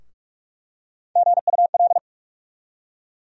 Morse code, heard as a keyed tone, about 46 wpm, 700 Hz. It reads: GUL
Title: USA Call Sign Postfixes - 3 Letters 45wpm